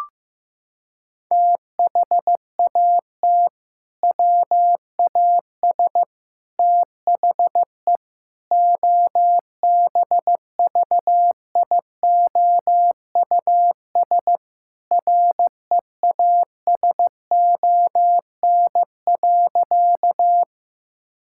Morse code, heard as a keyed tone, about 16 wpm, 700 Hz. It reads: THAT WAS THE OBVIOUS REASON.
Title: Sentences from Top 1600 Words 15wpm